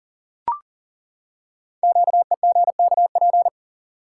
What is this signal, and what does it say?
Morse code, heard as a keyed tone, about 40 wpm, 700 Hz. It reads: QEGKP
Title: Code Groups - 5 Letters - 40wpm